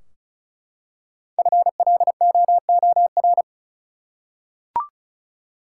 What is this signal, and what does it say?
Morse code, heard as a keyed tone, about 34 wpm, 700 Hz. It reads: FLOOR
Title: Five Letter Words - 35wpm